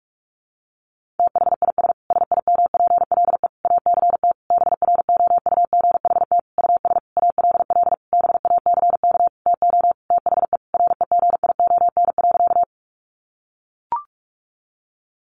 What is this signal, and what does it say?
Morse code, heard as a keyed tone, about 44 wpm, 700 Hz. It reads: THIS SIMPLE ACT BROUGHT US ALL BACK TO THE REGION.
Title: Sentences from Top 1500 Words 45wpm